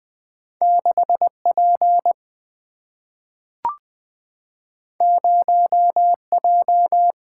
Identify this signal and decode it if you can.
Morse code, heard as a keyed tone, about 20 wpm, 700 Hz. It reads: 6P 0J
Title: Single-Letter Single-Number 20wpm